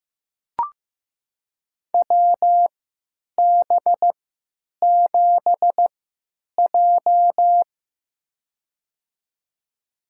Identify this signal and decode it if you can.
Morse code, heard as a keyed tone, about 15 wpm, spaced as for 9 wpm, 700 Hz. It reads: WB7J
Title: USA Call Signs - 4 Characters - 3x Character Spacing - 15wpm